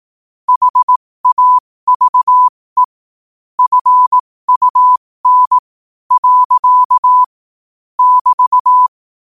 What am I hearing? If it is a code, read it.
Morse code, heard as a keyed tone, about 18 wpm, spaced as for 14 wpm, 1000 Hz. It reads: HAVE FUN. <BT>
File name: 15wpm